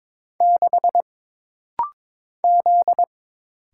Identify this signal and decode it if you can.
Morse code, heard as a keyed tone, about 22 wpm, 700 Hz. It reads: 6 Z
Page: This rapid-fire timing is geared to push your ICR (Instant Character Recognition) skills.